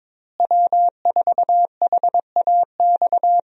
Morse code, heard as a keyed tone, about 22 wpm, 700 Hz. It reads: W4HAX
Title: USA Call Signs - 5 Characters - Repeated 3x - 22wpm